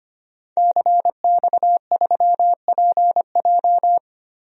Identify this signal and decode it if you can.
Morse code, heard as a keyed tone, about 25 wpm, 700 Hz. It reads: CX3PJ
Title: DXCC-100 Call Signs - 5 Characters - Repeated 3x - 25wpm